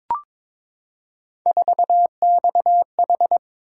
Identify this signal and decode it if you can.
Morse code, heard as a keyed tone, about 23 wpm, 700 Hz. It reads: 4XH
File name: practice / 22wpm / Three Letter-Number Combinations - 22wpm